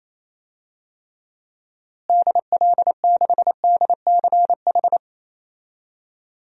Morse code, heard as a keyed tone, about 28 wpm, 700 Hz. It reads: DL6DCH
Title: DXCC-100 Call Signs - 6 Characters - Repeated 3x - 28wpm